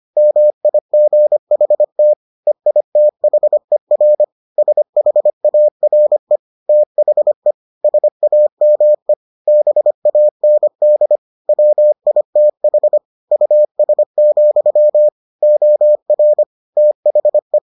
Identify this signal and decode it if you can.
Morse code, heard as a keyed tone, about 25 wpm, 600 Hz. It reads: MIGHT EITHER SHARE THE SAME BAND WITH US, OR THE